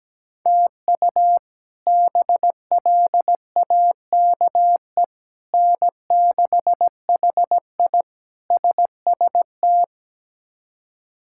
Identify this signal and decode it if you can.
Morse code, heard as a keyed tone, about 17 wpm, 700 Hz. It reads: TU BLAKE N6HI SST
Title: K1USN SST QSO Practice - 17wpm